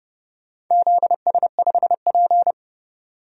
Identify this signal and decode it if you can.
Morse code, heard as a keyed tone, about 30 wpm, 700 Hz. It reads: ZS5P
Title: DXCC-100 Call Signs - 4 Characters - 30wpm